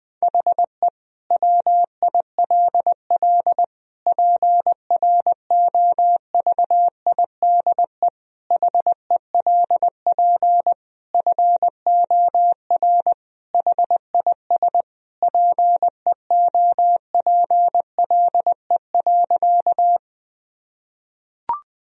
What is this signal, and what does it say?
Morse code, heard as a keyed tone, about 20 wpm, 700 Hz. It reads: HE WILL PROVIDE HELP FOR HIS PEOPLE.
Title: Sentences from Top 1000 Words - Review Part 3 - 20wpm